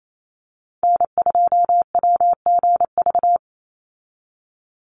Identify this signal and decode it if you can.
Morse code, heard as a keyed tone, about 28 wpm, 700 Hz. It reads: N2WGV
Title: USA Call Signs 28wpm